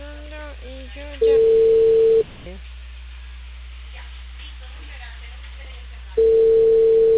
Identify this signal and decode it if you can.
Ringback tone (China, 450 Hz), with speech.